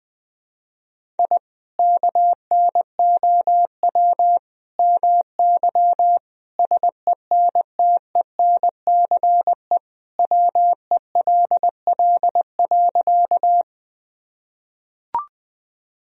Morse code, heard as a keyed tone, about 20 wpm, 700 Hz. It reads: I KNOW MY SENTENCE WELL.
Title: Sentences from Top 1000 Words - Review Part 1 - 20wpm